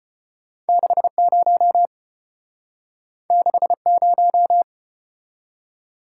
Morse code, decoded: 60 60

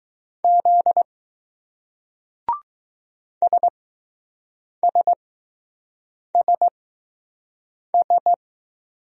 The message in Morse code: Z SSSS

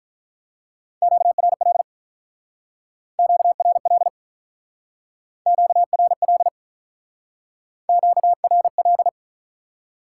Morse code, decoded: QRL QRL QRL QRL

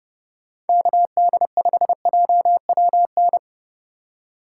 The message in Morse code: KD5JWN

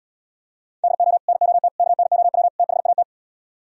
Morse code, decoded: DO NOT KNOW THAT

